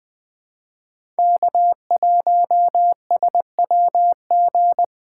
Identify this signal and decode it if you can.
Morse code, heard as a keyed tone, about 20 wpm, 700 Hz. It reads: K1SWG
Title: USA Call Signs - 3x Character Spacing - 20wpm